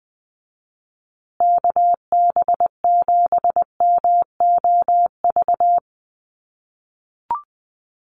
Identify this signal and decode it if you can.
Morse code, heard as a keyed tone, about 20 wpm, 700 Hz. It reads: KB7MOV
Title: USA Call Signs Repeated 3x 20wpm